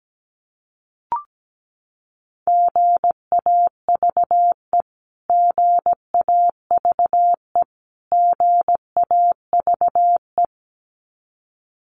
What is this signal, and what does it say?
Morse code, heard as a keyed tone, about 17 wpm, 700 Hz. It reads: GAVE GAVE GAVE
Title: Top 500 Words Repeated 17wpm